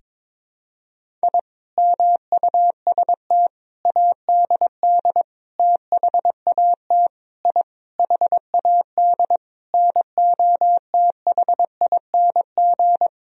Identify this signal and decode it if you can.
Morse code, heard as a keyed tone, about 22 wpm, 700 Hz. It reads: I MUST ADD THAT I HAD NOTHING
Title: Sentences from Top 1000 Words - Review Part 1 - 22wpm